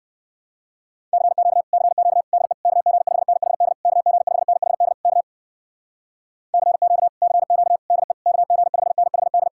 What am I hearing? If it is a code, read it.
Morse code, heard as a keyed tone, about 58 wpm, 700 Hz. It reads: CQ CQ DE KG5NHD KG5NHD K CQ CQ DE KG5NHD